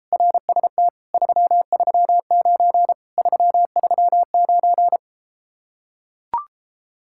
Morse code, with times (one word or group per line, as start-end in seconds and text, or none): 0.1-0.9 RST
1.1-2.9 339
3.2-5.0 339